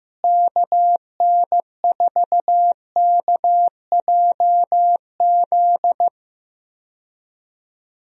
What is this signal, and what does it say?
Morse code, heard as a keyed tone, about 15 wpm, 700 Hz. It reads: KN4KJZ
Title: USA Call Signs - 6 Characters - Repeated 2x - 15wpm